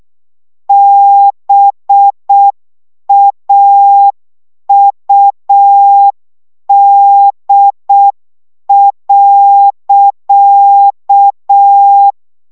Morse code, heard as a keyed tone, about 6 wpm, 800 Hz. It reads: BAUD.
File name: szamolo / CW 5 baud